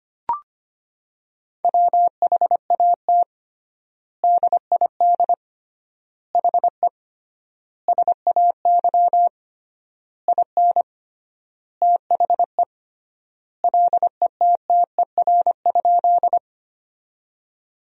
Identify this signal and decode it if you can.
Morse code, heard as a keyed tone, about 25 wpm, 700 Hz. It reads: WHAT DID HE SAY IN THE LETTER?
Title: 3x Word-Spacing - Sentences from Top 300 Words - 25wpm